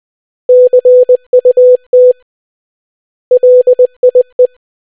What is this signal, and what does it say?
Morse code, heard as a keyed tone, about 20 wpm, 500 Hz. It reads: CUT LIE